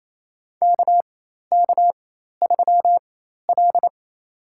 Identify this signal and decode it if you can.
Morse code, heard as a keyed tone, about 28 wpm, spaced as for 13 wpm, 700 Hz. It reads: KK3L
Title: USA Call Signs - 4 Characters - 4x Character Spacing - 28wpm